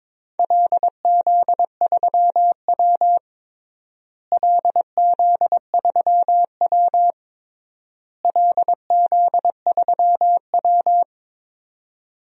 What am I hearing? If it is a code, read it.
Morse code, heard as a keyed tone, about 22 wpm, 700 Hz. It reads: LZ3W LZ3W LZ3W
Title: DXCC-100 Call Signs - 4 Characters - Repeated 3x - 22wpm